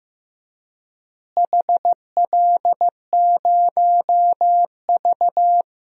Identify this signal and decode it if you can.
Morse code, heard as a keyed tone, about 15 wpm, 700 Hz. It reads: HL0V
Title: DXCC-100 Call Signs - 4 Characters - Repeated 3x - 15wpm